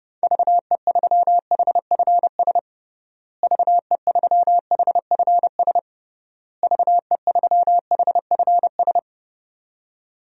Morse code, heard as a keyed tone, about 29 wpm, 700 Hz. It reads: VE3HFS VE3HFS VE3HFS
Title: DXCC-100 Call Signs - 6 Characters - Repeated 3x - 30wpm